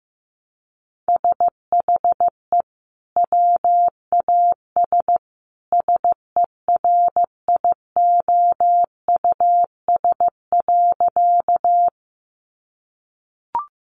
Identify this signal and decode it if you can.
Morse code, heard as a keyed tone, about 15 wpm, 700 Hz. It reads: SHE WAS SERIOUS.
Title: Sentences from Top 1100 Words 15wpm